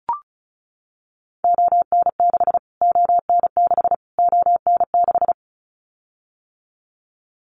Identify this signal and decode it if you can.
Morse code, heard as a keyed tone, about 34 wpm, 700 Hz. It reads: ON6 ON6 ON6
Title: Callsign Prefixes for Top 25 DXCC - Repeated 3x - 35wpm